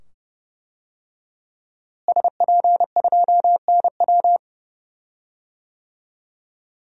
Morse code, heard as a keyed tone, about 31 wpm, 700 Hz. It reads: SP2NW
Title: DXCC-100 Call Signs - 5 Characters - Repeated 2x - 30wpm